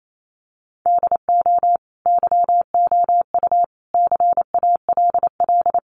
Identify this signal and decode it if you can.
Morse code, heard as a keyed tone, about 28 wpm, 700 Hz. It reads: DO YOU CALL
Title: Sentences from Top 1500 Words 28wpm